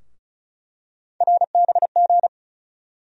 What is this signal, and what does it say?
Morse code, heard as a keyed tone, about 36 wpm, 700 Hz. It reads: RBG
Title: USA Call Sign Postfixes - 3 Letters 35wpm